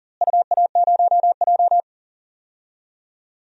Morse code, heard as a keyed tone, about 40 wpm, 700 Hz. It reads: UA0J
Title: DXCC-100 Call Signs - 4 Characters - Repeated 3x - 40wpm